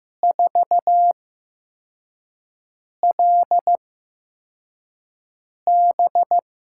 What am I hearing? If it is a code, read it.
Morse code, heard as a keyed tone, about 15 wpm, spaced as for 4 wpm, 700 Hz. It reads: 4LB
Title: USA Call Signs - 6 Characters - 8x Character Spacing - 15wpm